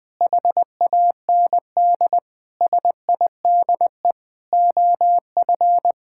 Morse code, heard as a keyed tone, about 20 wpm, 700 Hz. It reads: HAND SIDE OF
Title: Sentences with 7-Word Phrases - 20wpm